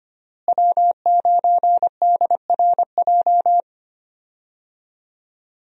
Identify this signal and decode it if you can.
Morse code, heard as a keyed tone, about 25 wpm, 700 Hz. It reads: W9DRJ
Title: USA Call Signs - 5 Characters - Repeated 3x - 25wpm